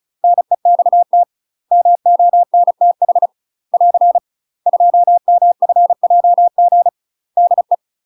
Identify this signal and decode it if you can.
Morse code, heard as a keyed tone, about 35 wpm, 700 Hz. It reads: NEXT MONTH <AR> 2MFJG DE
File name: UK MASTER QSOs TRAINING / 35WPM / UK MASTER QSOs 0034